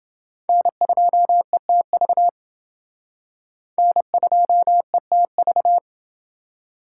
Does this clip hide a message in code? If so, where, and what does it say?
Yes: Morse code from 0.5 to 5.8 s, heard as a keyed tone, about 29 wpm, 700 Hz. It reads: N2ETV N2ETV